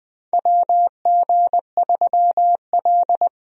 Morse code, heard as a keyed tone, about 20 wpm, 700 Hz. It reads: WG3L
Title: USA Call Signs - 4 Characters - Repeated 3x - 20wpm